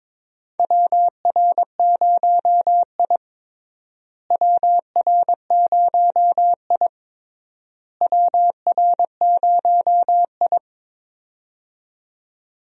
Morse code, heard as a keyed tone, about 22 wpm, 700 Hz. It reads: WR0I WR0I WR0I
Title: USA Call Signs - 4 Characters - Repeated 3x - 22wpm